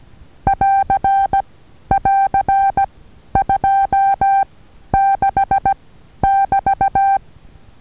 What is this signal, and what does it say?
Morse code, heard as a keyed tone, about 16 wpm, 790 Hz. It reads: <AR> <AR> 2 6 <BT>
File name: Taviro HA6VA A